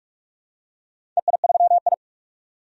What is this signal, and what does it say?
Morse code, heard as a keyed tone, about 46 wpm, 700 Hz. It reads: EI3I
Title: DXCC-100 Call Signs - 4 Characters - 45wpm